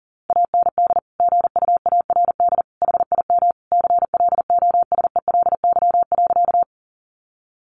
Morse code, heard as a keyed tone, about 40 wpm, 700 Hz. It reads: AND GUARD HIM CLOSELY.
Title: Sentences from Top 1400 Words 40wpm